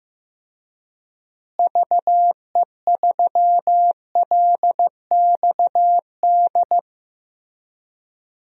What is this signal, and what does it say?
Morse code, heard as a keyed tone, about 15 wpm, 700 Hz. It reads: VE3LXD
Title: DXCC-100 Call Signs - 6 Characters - Repeated 3x - 15wpm